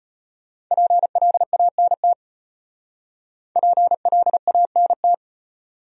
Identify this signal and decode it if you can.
Morse code, heard as a keyed tone, about 33 wpm, 700 Hz. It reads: PLANT PLANT